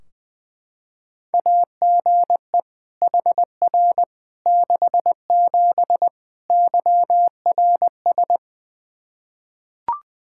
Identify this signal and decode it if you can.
Morse code, heard as a keyed tone, about 20 wpm, 700 Hz. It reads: AGE HR 67 YRS
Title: Practice QSO - Exchange 3 - More Info - 20wpm